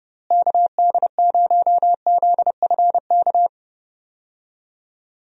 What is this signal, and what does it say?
Morse code, heard as a keyed tone, about 30 wpm, 700 Hz. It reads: KD0ZFK
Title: USA Call Signs - 6 Characters - Repeated 3x - 30wpm